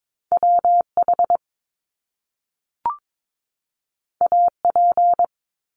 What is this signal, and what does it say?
Morse code, heard as a keyed tone, about 22 wpm, 700 Hz. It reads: WH AP
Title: Top Two Letter Combinations in Words 22wpm